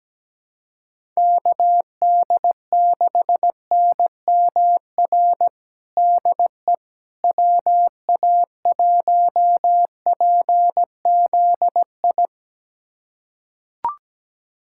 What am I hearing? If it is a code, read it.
Morse code, heard as a keyed tone, about 17 wpm, 700 Hz. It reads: KD6NMR DE WA1PZI
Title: USA Callsign DE Callsign 17wpm